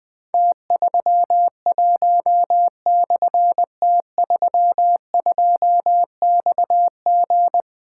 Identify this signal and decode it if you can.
Morse code, heard as a keyed tone, about 21 wpm, 700 Hz. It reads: T31/T32XG
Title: Difficult Call Signs 20wpm